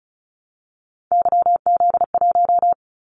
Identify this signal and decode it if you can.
Morse code, heard as a keyed tone, about 36 wpm, 700 Hz. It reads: YZ1
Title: Callsign Prefixes for Top 25 DXCC - 35wpm